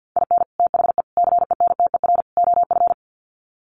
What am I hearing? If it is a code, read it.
Morse code, heard as a keyed tone, about 50 wpm, 700 Hz. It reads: IN THE CENTER OF
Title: Sets of 4 Words 50wpm